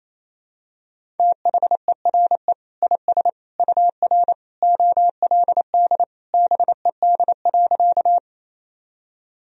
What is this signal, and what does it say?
Morse code, heard as a keyed tone, about 28 wpm, 700 Hz. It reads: THERE IS UR OLD BED.
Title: Sentences from Top 500 Words and Abbreviations 28wpm